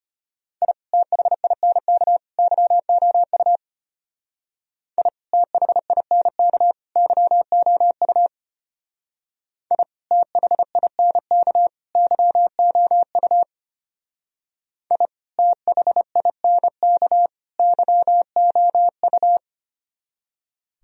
Morse code, heard as a keyed tone, about 32 wpm, 700 Hz. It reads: I THINK YOU I THINK YOU I THINK YOU I THINK YOU